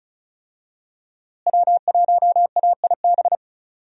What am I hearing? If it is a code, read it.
Morse code, heard as a keyed tone, about 36 wpm, 700 Hz. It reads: W1AIB
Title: USA Call Signs - 5 Characters - Repeated 2x - 35wpm